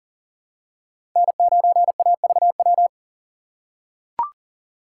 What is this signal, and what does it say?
Morse code, heard as a keyed tone, about 40 wpm, 700 Hz. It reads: N9AVW
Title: USA Call Signs - 5 Characters - 6x Character Spacing - 40wpm